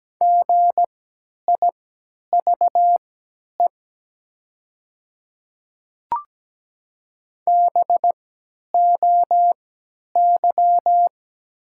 Morse code, heard as a keyed tone, about 17 wpm, spaced as for 10 wpm, 700 Hz. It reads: GIVE BOY
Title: 3x Character Spacing - Top 200 Words - 17wpm